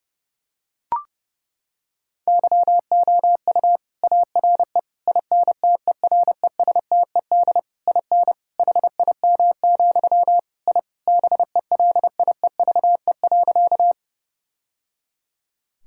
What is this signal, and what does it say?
Morse code, heard as a keyed tone, about 31 wpm, 700 Hz. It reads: YOU ARE INTERESTED IN HIM, I BELIEVE.
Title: Sentences from Top 1000 Words - Review Part 6 - 30wpm